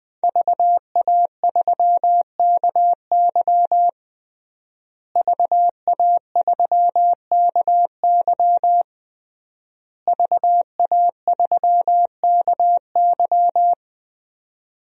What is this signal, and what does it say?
Morse code, heard as a keyed tone, about 20 wpm, 700 Hz. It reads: VA3KY VA3KY VA3KY
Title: DXCC-100 Call Signs - 5 Characters - Repeated 3x - 20wpm